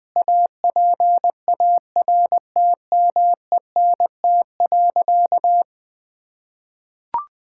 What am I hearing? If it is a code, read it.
Morse code, heard as a keyed tone, about 20 wpm, 700 Hz. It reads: APARTMENT.
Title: Sentences from Top 1000 Words - Review Part 6 - 20wpm